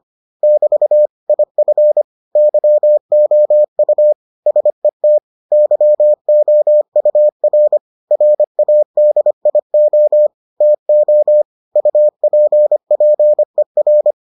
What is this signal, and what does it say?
Morse code, heard as a keyed tone, about 25 wpm, 600 Hz. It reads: <BT> IF YOU SET YOUR RADIO TO UPPER